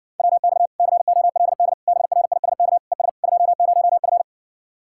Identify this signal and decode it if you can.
Morse code, heard as a keyed tone, about 56 wpm, 700 Hz. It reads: WX COLD BRISK ES 20F